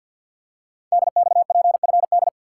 Morse code, heard as a keyed tone, about 50 wpm, 700 Hz. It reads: DXPFD
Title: Code Groups - 5 Letters - 50wpm